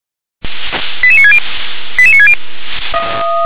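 Five-tone selective call (ZVEI), heard as quick runs of five tones, tones of about 70 ms. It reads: 88960 88960
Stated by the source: ZVEI five-tone selective call received with an SDR.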